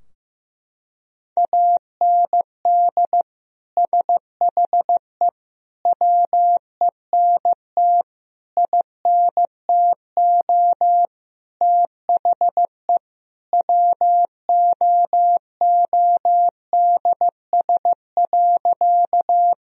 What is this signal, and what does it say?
Morse code, heard as a keyed tone, about 15 wpm, 700 Hz. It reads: AND SHE WENT INTO THE WOODS.
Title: Sentences from Top 1000 Words - Review Part 2 - 15wpm